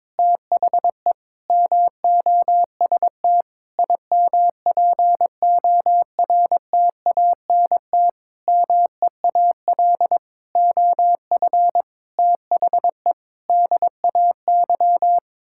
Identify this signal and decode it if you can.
Morse code, heard as a keyed tone, about 22 wpm, 700 Hz. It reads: THE MOST IMPORTANT MEAL OF THE DAY
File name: Sets of 9 Words - 22wpm